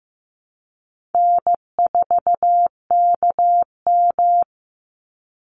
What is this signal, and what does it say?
Morse code, heard as a keyed tone, about 15 wpm, 700 Hz. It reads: N4KM
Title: Easy Call Signs - 15wpm